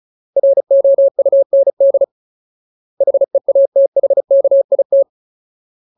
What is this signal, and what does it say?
Morse code, heard as a keyed tone, about 34 wpm, 550 Hz. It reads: ROUND HEATHKIT
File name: Ultimate Word List / 35WPM Ultimate